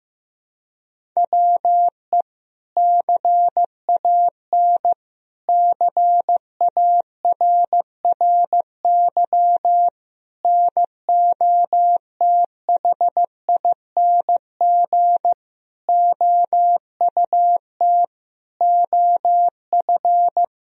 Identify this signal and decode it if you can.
Morse code, heard as a keyed tone, about 15 wpm, 700 Hz. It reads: WE CAN CARRY NOTHING OUT OF
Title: Sentences from Top 1000 Words - Review Part 1 - 15wpm